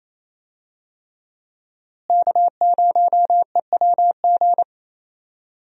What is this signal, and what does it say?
Morse code, heard as a keyed tone, about 28 wpm, 700 Hz. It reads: K0EWG
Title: USA Call Signs - 5 Characters - Repeated 3x - 28wpm